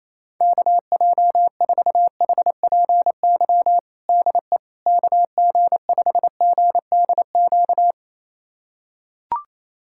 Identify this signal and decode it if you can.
Morse code, heard as a keyed tone, about 28 wpm, 700 Hz. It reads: KJ4HPY DE KG5GDQ
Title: USA Callsign DE Callsign 28wpm